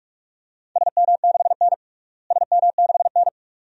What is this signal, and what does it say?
Morse code, heard as a keyed tone, about 47 wpm, 700 Hz. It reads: SM6N SM6N
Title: DXCC-100 Call Signs - 4 Characters - Repeated 2x - 45wpm